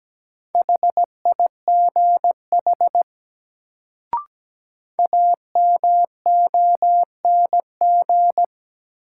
Morse code, heard as a keyed tone, about 17 wpm, 700 Hz. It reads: HIGH AMONG
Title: Sets of 2 Words - New Word Drill - 17wpm